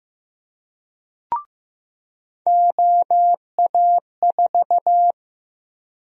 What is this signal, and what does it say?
Morse code, heard as a keyed tone, about 15 wpm, 700 Hz. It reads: OA4
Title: USA Call Sign Prefixes - 15wpm